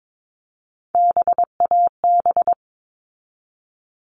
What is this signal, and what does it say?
Morse code, heard as a keyed tone, about 22 wpm, 700 Hz. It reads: BAB
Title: All Cut Numbers - 3-Digit 22wpm